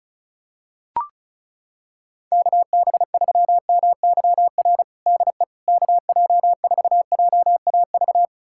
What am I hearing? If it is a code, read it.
Morse code, heard as a keyed tone, about 34 wpm, 700 Hz. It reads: KB3MYR DE KJ4JAV
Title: USA Callsign DE Callsign - 6 Characters - 35wpm